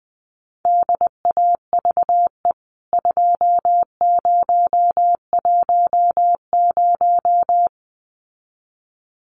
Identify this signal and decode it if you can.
Morse code, heard as a keyed tone, about 20 wpm, 700 Hz. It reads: DAVE 2010
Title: CWT Exchange 20wpm